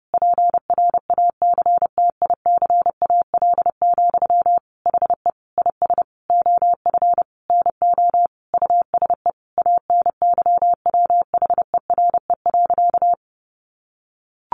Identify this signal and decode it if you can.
Morse code, heard as a keyed tone, about 30 wpm, 700 Hz. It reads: PRACTICAL, HE IS OF NO USE ANYWHERE.
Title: Sentences from Top 1400 Words 30wpm